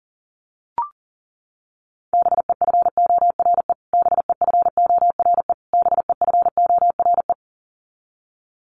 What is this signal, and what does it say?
Morse code, heard as a keyed tone, about 40 wpm, 700 Hz. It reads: BEFORE BEFORE BEFORE
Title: Top 300 Words Repeated 40wpm